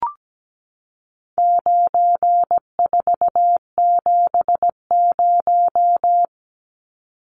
Morse code, heard as a keyed tone, about 17 wpm, 700 Hz. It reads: 9470
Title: Numbers - 4-Digit 17wpm